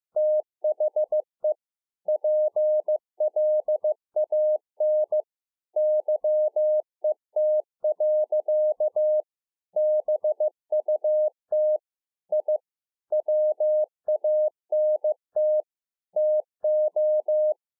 Morse code, heard as a keyed tone, about 15 wpm, 610 Hz. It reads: THE PLAN YET. BUT I WANT TO